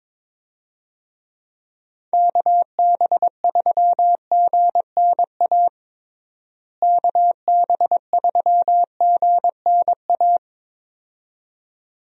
Morse code, heard as a keyed tone, about 22 wpm, 700 Hz. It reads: KB3GNA KB3GNA
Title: USA Call Signs - 6 Characters - Repeated 2x - 22wpm